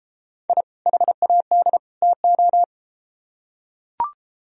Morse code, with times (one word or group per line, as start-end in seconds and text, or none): 0.5-0.6 I
0.9-1.8 HAD
2.0-2.6 TO